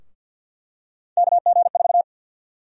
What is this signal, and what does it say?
Morse code, heard as a keyed tone, about 50 wpm, 700 Hz. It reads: KG4
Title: USA Call Sign Prefixes - Repeated 2x - 50wpm